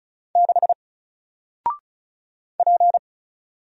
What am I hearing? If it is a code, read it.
Morse code, heard as a keyed tone, about 36 wpm, 700 Hz. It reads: 6 P
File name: Single-Letter-Number-35wpm